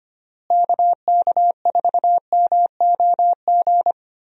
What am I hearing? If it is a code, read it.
Morse code, heard as a keyed tone, about 25 wpm, 700 Hz. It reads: KK4MOG
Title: USA Call Signs - 6 Characters - Repeated 3x - 25wpm